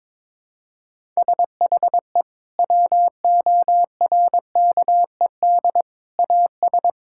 Morse code, heard as a keyed tone, about 22 wpm, 700 Hz. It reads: SHE WORKED AS
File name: practice / 22wpm / Sentences with 4-Word Phrases - 22wpm